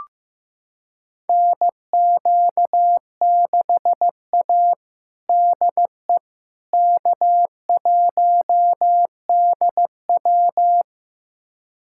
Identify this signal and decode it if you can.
Morse code, heard as a keyed tone, about 15 wpm, 700 Hz. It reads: NQ6A DE K1DW
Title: USA Callsign DE Callsign - 4 Characters - 15wpm